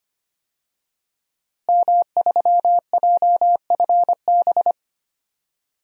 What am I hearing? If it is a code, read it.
Morse code, heard as a keyed tone, about 25 wpm, 700 Hz. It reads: M3JFB
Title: DXCC-100 Call Signs - 5 Characters - Repeated 2x - 25wpm